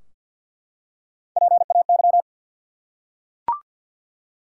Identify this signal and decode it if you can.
Morse code, heard as a keyed tone, about 52 wpm, 700 Hz. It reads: PAY
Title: Three Letter Words - 50wpm